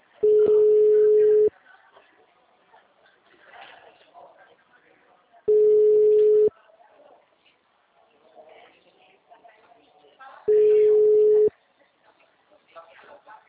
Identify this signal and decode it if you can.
Ringback tone (Europe, 425 Hz), with speech.